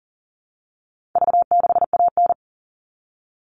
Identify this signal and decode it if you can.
Morse code, heard as a keyed tone, about 37 wpm, 700 Hz. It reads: V6AN
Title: All Cut Numbers - 4-Digit 40wpm